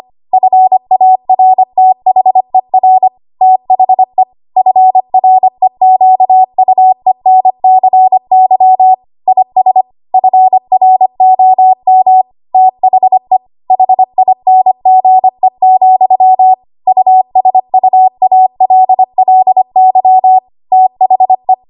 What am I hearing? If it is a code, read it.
Morse code, heard as a keyed tone, about 25 wpm, 750 Hz. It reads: FARTHER THE FREQUENCY IS FROM THE HINGE, USUALLY THE
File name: Morse / 25WPM